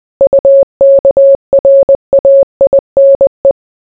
Morse code, heard as a keyed tone, about 20 wpm, 550 Hz. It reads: UKRAINE